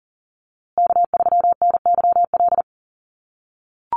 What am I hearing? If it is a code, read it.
Morse code, heard as a keyed tone, about 40 wpm, 700 Hz. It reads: K3NYL
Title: USA Call Signs Repeated 2x 40wpm